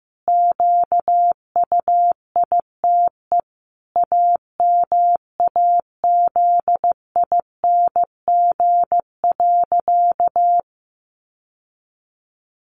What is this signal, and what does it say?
Morse code, heard as a keyed tone, about 15 wpm, 700 Hz. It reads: QUITE AMAZING.
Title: Sentences from Top 1500 Words 15wpm